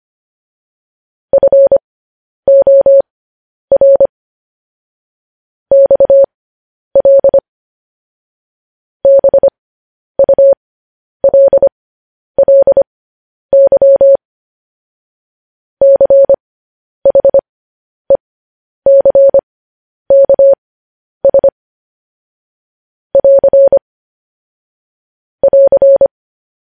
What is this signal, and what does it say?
Morse code, heard as a keyed tone, about 25 wpm, spaced as for 10 wpm, 550 Hz. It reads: FOR XL BULLY CHECKS <AR> <AR>
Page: news-headlines-in-morse-code-at-10-wpm